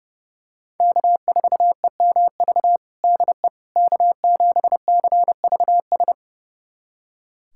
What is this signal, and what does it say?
Morse code, heard as a keyed tone, about 30 wpm, 700 Hz. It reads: K4EMV DE K7CVS
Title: USA Callsign DE Callsign - 5 Characters - 30wpm